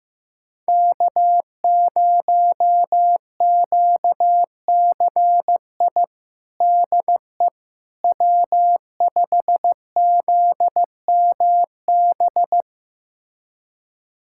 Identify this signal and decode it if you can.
Morse code, heard as a keyed tone, about 15 wpm, 700 Hz. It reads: K0QCI DE W5ZMB
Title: USA Callsign DE Callsign - 5 Characters - 15wpm